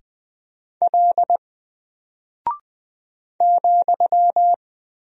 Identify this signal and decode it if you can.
Morse code, heard as a keyed tone, about 20 wpm, 700 Hz. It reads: L ,
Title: Single Character (Letter:Number:Punctuation:Prosign) 20wpm